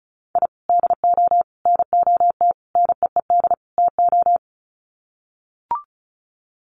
Morse code, heard as a keyed tone, about 34 wpm, 700 Hz. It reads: I DO NOT NEED TO
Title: Sets of 5 Words 35wpm